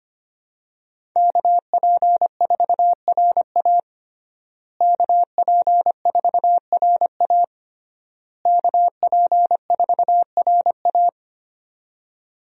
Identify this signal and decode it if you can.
Morse code, heard as a keyed tone, about 25 wpm, 700 Hz. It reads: KP4RA KP4RA KP4RA
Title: DXCC-100 Call Signs - 5 Characters - Repeated 3x - 25wpm